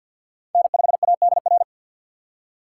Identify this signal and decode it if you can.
Morse code, heard as a keyed tone, about 50 wpm, 700 Hz. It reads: N5ADR
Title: USA Call Signs - 5 Characters - 4x Character Spacing - 50wpm